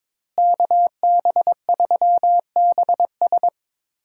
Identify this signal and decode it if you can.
Morse code, heard as a keyed tone, about 22 wpm, 700 Hz. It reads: KB3BS
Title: USA Call Signs - 5 Characters - Repeated 3x - 22wpm